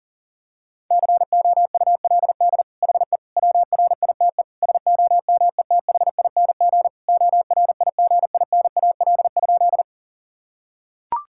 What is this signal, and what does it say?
Morse code, heard as a keyed tone, about 40 wpm, 700 Hz. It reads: COULD HE WRITE SOMETHING ORIGINAL?
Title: Sentences from Top 1000 Words - Review Part 6 - 40wpm